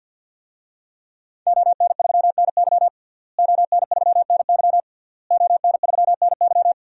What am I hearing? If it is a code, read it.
Morse code, heard as a keyed tone, about 52 wpm, 700 Hz. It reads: ON3NY ON3NY ON3NY
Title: DXCC-100 Call Signs - 5 Characters - Repeated 3x - 50wpm